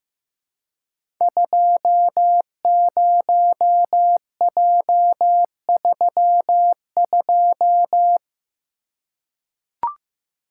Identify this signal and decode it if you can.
Morse code, heard as a keyed tone, about 15 wpm, 700 Hz. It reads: 20J32
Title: Code Groups - 5 Letter:Numbers - 15wpm